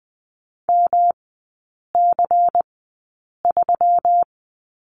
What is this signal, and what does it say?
Morse code, heard as a keyed tone, about 20 wpm, spaced as for 8 wpm, 700 Hz. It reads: MC3
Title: 145 Lesson 24e - Letter y - Three Characters 20wpm